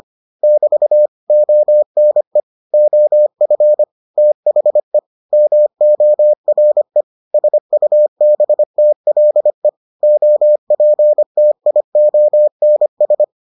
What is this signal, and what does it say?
Morse code, heard as a keyed tone, about 25 wpm, 600 Hz. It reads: <BT> ONE OF THE MORE SUBTLE OPTIONS